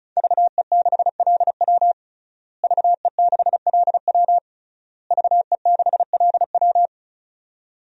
Morse code, heard as a keyed tone, about 36 wpm, 700 Hz. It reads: VE6LW VE6LW VE6LW
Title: DXCC-100 Call Signs - 5 Characters - Repeated 3x - 35wpm